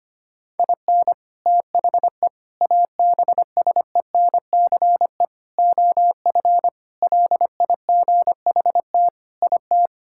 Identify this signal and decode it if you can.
Morse code, heard as a keyed tone, about 25 wpm, 700 Hz. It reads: IN THE ABSENCE OF LIGHT IT
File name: practice / 25wpm / Sentences with 4-Word Phrases - 25wpm